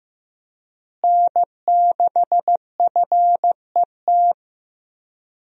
Morse code, heard as a keyed tone, about 15 wpm, 700 Hz. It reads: N6FET
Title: USA Call Signs - 5 Characters - Repeated 2x - 15wpm